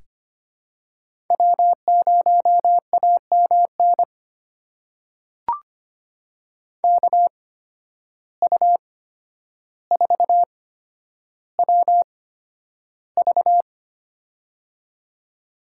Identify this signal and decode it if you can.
Morse code, heard as a keyed tone, about 25 wpm, spaced as for 11 wpm, 700 Hz. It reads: W0AMN KU4WV
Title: USA Call Signs - 5 Characters - 8x Character Spacing - 25wpm